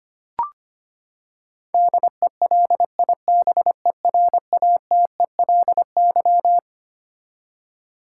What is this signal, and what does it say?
Morse code, heard as a keyed tone, about 25 wpm, 700 Hz. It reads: DELIBERATELY
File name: Twelve Letter Words - 25wpm